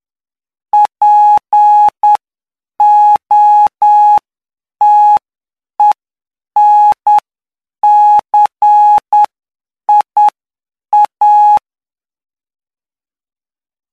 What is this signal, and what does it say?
Morse code, heard as a keyed tone, about 9 wpm, spaced as for 7 wpm, 840 Hz. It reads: POTENCIA
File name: cw / py2lcd / 12-PY2LCD-Li��o 21